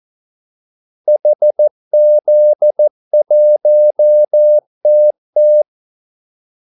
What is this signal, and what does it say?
Morse code, heard as a keyed tone, about 14 wpm, 600 Hz. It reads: HZ1TT
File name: Callsigns-14wpm